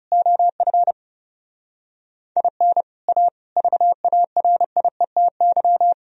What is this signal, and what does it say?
Morse code, heard as a keyed tone, about 30 wpm, 700 Hz. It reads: OF IN A VARIETY
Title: Speed-Racing - Sets of 4 Words 30wpm